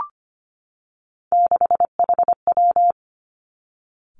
Morse code, heard as a keyed tone, about 25 wpm, 700 Hz. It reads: 6HW
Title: Three Challenging Letter-Number Combinations 25wpm